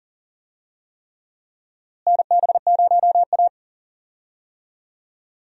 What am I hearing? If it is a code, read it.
Morse code, heard as a keyed tone, about 40 wpm, 700 Hz. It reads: NB0A
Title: USA Call Signs - 4 Characters - Repeated 3x - 40wpm